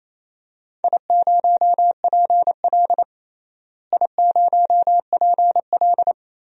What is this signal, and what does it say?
Morse code, heard as a keyed tone, about 27 wpm, 700 Hz. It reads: I0PL I0PL
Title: DXCC-100 Call Signs - 4 Characters - Repeated 3x - 28wpm